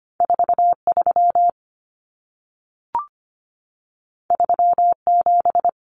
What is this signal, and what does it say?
Morse code, heard as a keyed tone, about 25 wpm, 700 Hz. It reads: 43 37
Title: Common Cut Numbers - 2-Digit 25wpm